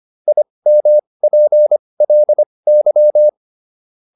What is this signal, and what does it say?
Morse code, heard as a keyed tone, about 25 wpm, spaced as for 20 wpm, 600 Hz. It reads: IMPLY